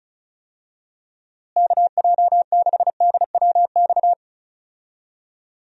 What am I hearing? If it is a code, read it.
Morse code, heard as a keyed tone, about 36 wpm, 700 Hz. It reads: KJ6DWX
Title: USA Call Signs - 6 Characters - Repeated 3x - 35wpm